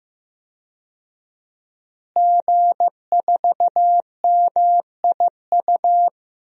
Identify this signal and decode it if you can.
Morse code, heard as a keyed tone, about 15 wpm, 700 Hz. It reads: G4MIU
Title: DXCC-100 Call Signs - 5 Characters - Repeated 2x - 15wpm